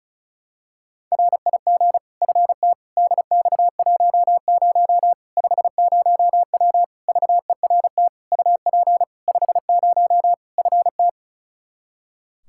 Morse code, heard as a keyed tone, about 34 wpm, 700 Hz. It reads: RIG FT DX10 50W VERT UP 50 FT
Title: Practice QSO - Exchange 2 - Info - 35wpm